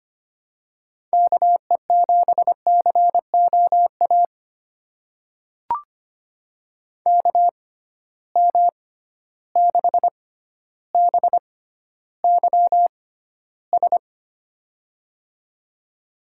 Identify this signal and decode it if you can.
Morse code, heard as a keyed tone, about 25 wpm, spaced as for 13 wpm, 700 Hz. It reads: KE7COA KM6BYS
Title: USA Call Signs - 6 Characters - 6x Character Spacing - 25wpm